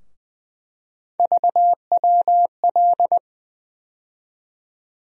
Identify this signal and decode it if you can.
Morse code, heard as a keyed tone, about 20 wpm, 700 Hz. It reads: VWL
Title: USA Call Sign Postfixes - 3 Letters 20wpm